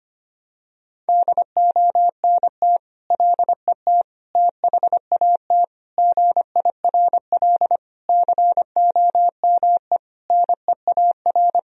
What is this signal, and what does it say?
Morse code, heard as a keyed tone, about 25 wpm, 700 Hz. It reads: DONT LET THAT GIRL COME NEAR
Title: Sentences from Top 1000 Words - Review Part 1 - 25wpm